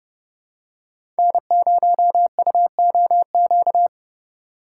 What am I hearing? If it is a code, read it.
Morse code, heard as a keyed tone, about 30 wpm, 700 Hz. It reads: N0UOQ
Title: USA Call Signs - 8x Character Spacing - 30wpm